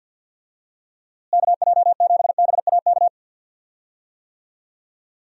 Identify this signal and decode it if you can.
Morse code, heard as a keyed tone, about 50 wpm, 700 Hz. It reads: KJ7BAK
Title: USA Call Signs - 6 Characters - Repeated 2x - 50wpm